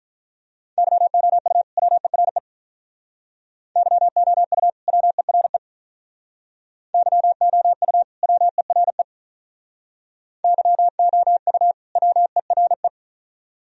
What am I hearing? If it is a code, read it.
Morse code, heard as a keyed tone, about 44 wpm, 700 Hz. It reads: YOU WERE YOU WERE YOU WERE YOU WERE
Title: Speed-Racing - Sets of 2 Words 35wpm